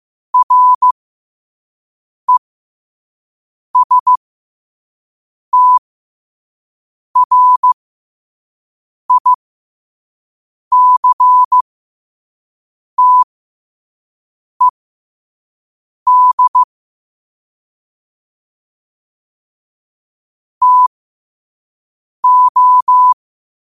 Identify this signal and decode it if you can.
Morse code, heard as a keyed tone, about 15 wpm, spaced as for 5 wpm, 1000 Hz. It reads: RESTRICTED TO
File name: qotdfiles / 5wpm